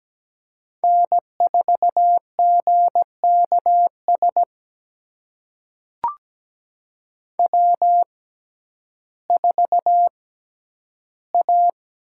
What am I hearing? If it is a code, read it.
Morse code, heard as a keyed tone, about 17 wpm, spaced as for 10 wpm, 700 Hz. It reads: N4GKS W4A
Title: USA Call Signs - 5 Characters - 6x Character Spacing - 17wpm